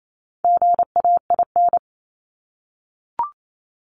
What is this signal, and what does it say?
Morse code, heard as a keyed tone, about 28 wpm, 700 Hz. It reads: GAIN
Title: Top 751-1000 Words 28wpm